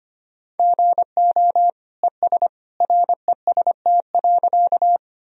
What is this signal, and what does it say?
Morse code, heard as a keyed tone, about 25 wpm, 700 Hz. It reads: GO ES REST.
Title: Sentences from Top 500 Words and Abbreviations 25wpm